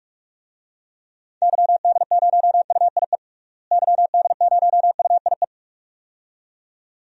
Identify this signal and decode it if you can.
Morse code, heard as a keyed tone, about 46 wpm, 700 Hz. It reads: YD0UIE YD0UIE
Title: DXCC-100 Call Signs - 6 Characters - Repeated 2x - 45wpm